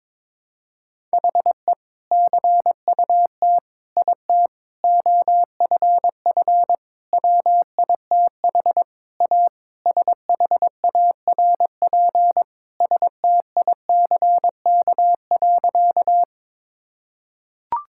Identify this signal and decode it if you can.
Morse code, heard as a keyed tone, about 21 wpm, 700 Hz. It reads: HE CUT IT OFF WITH A SHARP STICK.
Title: Sentences from Top 1600 Words 22wpm